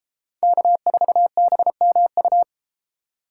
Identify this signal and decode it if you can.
Morse code, heard as a keyed tone, about 33 wpm, 700 Hz. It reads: K4BMU